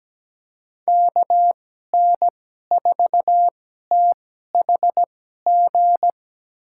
Morse code, heard as a keyed tone, about 17 wpm, spaced as for 12 wpm, 700 Hz. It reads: KN4THG
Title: USA Call Signs - 6 Characters - 2x Character Spacing - 17wpm